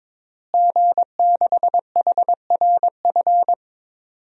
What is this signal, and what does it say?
Morse code, heard as a keyed tone, about 22 wpm, 700 Hz. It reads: G6HRF
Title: DXCC-100 Call Signs - 5 Characters - Repeated 2x - 22wpm